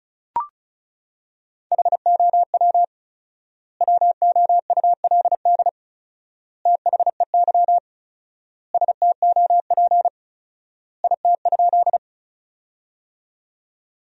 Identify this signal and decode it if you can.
Morse code, heard as a keyed tone, about 36 wpm, 700 Hz. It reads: HOW WOULD THEY STOP IT?
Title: 4x Word-Spacing - Sentences from Top 300 Words - 35wpm